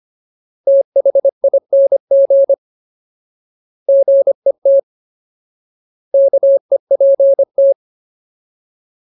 Morse code, heard as a keyed tone, about 25 wpm, 550 Hz. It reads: THING GET KEPT
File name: Ultimate Word List / 25WPM Ultimate